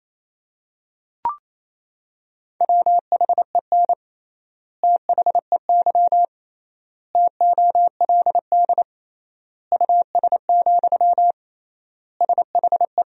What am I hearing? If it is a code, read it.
Morse code, heard as a keyed tone, about 28 wpm, 700 Hz. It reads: WHEN THEY TOLD US, SHE
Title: 3x Word-Spacing - Sentences from Top 400 Words - 28wpm